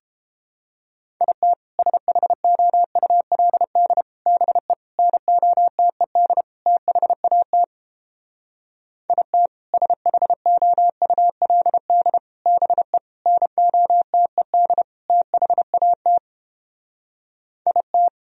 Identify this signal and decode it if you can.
Morse code, heard as a keyed tone, about 29 wpm, 700 Hz. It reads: IT SHOULD BE NOTED THAT IT SHOULD BE NOTED THAT IT